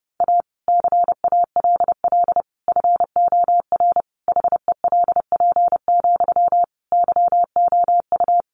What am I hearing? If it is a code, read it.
Morse code, heard as a keyed tone, about 30 wpm, 700 Hz. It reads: A CALL FOR HELP, YOU
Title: Sentences from Top 300 Words 30wpm